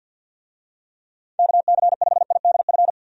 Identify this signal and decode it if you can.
Morse code, heard as a keyed tone, about 51 wpm, 700 Hz. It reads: KC5IDF
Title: USA Call Signs - 6 Characters - 50wpm